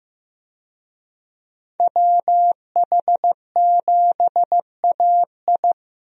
Morse code, heard as a keyed tone, about 15 wpm, 700 Hz. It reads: WH7AI
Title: USA Call Signs - 5 Characters - Repeated 2x - 15wpm